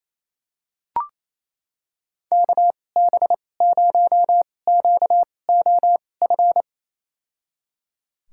Morse code, heard as a keyed tone, about 28 wpm, spaced as for 20 wpm, 700 Hz. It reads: KB0QOF
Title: USA Call Signs - 6 Characters - 2x Character Spacing - 28wpm